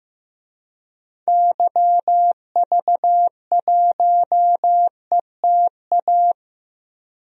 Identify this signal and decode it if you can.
Morse code, heard as a keyed tone, about 15 wpm, 700 Hz. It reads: YV1ETA
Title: DXCC-100 Call Signs - 6 Characters - 15wpm